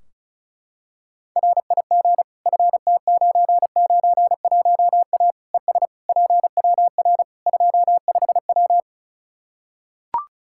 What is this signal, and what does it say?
Morse code, heard as a keyed tone, about 34 wpm, 700 Hz. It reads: RIG FT991A ES PWR 25W
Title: Practice QSO - Exchange 2 - Info - 35wpm